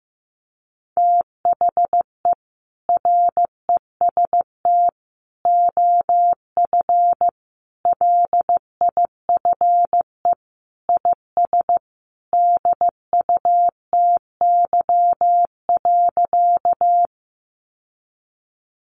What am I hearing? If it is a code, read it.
Morse code, heard as a keyed tone, about 15 wpm, 700 Hz. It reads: THE REST OF LIFE IS DUTY.
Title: Sentences from Top 1500 Words 15wpm